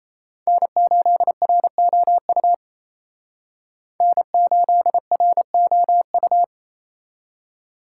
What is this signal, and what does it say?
Morse code, heard as a keyed tone, about 35 wpm, 700 Hz. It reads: N8ROU N8ROU